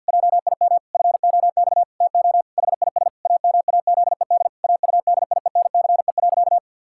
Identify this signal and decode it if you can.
Morse code, heard as a keyed tone, about 50 wpm, 700 Hz. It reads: JIM FOX TO HIS AMAZED AUDIENCE.
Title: Pangram Challenge v2 - 50wpm